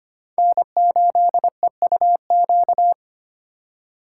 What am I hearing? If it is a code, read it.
Morse code, heard as a keyed tone, about 25 wpm, 700 Hz. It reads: N8EUQ
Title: USA Call Signs - 5 Characters - 2x Character Spacing - 25wpm